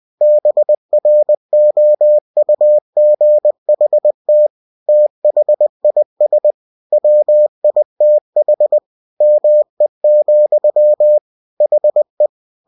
Morse code, heard as a keyed tone, about 20 wpm, 600 Hz. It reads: BROUGHT THIS WITH ME, HE